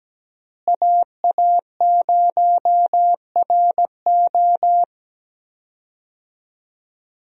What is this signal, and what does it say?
Morse code, heard as a keyed tone, about 17 wpm, 700 Hz. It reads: AA0RO
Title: USA Call Signs - 5 Characters - Repeated 3x - 17wpm